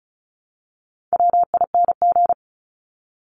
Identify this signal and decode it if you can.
Morse code, heard as a keyed tone, about 34 wpm, 700 Hz. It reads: WING
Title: Top 1001-1250 Words 35wpm